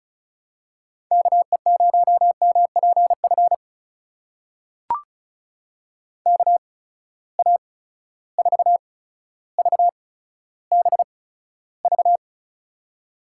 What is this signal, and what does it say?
Morse code, heard as a keyed tone, about 34 wpm, spaced as for 15 wpm, 700 Hz. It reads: KE0MPF KA4VBV
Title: USA Call Signs - 6 Characters - 8x Character Spacing - 35wpm